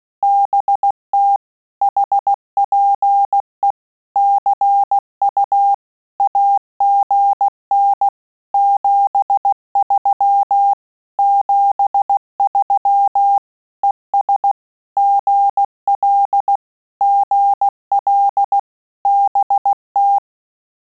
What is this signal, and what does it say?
Morse code, heard as a keyed tone, about 16 wpm, 800 Hz. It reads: BT HPE CU AGN 73 73 ES GL GL BT
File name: QSO CW 75 CPM